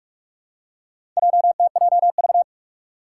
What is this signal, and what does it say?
Morse code, heard as a keyed tone, about 46 wpm, 700 Hz. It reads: JTJV
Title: Four Challenging Letter-Number Combinations 45wpm